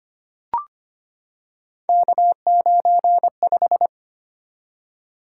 Morse code, heard as a keyed tone, about 25 wpm, 700 Hz. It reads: K95
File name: Three Letter-Number Combinations - 25wpm